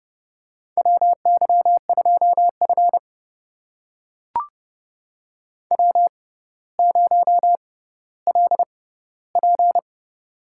Morse code, heard as a keyed tone, about 30 wpm, spaced as for 15 wpm, 700 Hz. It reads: WY2F W0LP
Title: USA Call Signs - 6x Character Spacing - 30wpm